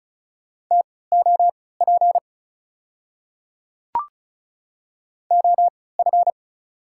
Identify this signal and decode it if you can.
Morse code, heard as a keyed tone, about 36 wpm, spaced as for 20 wpm, 690 Hz. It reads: TOP OF
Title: 3x Character Spacing - Top 500 Words - 35wpm